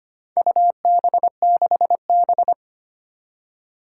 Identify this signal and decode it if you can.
Morse code, heard as a keyed tone, about 25 wpm, 700 Hz. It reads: UB6B
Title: Four Challenging Letter-Number Combinations 25wpm